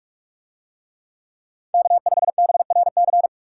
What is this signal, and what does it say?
Morse code, heard as a keyed tone, about 44 wpm, 700 Hz. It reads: K5BRC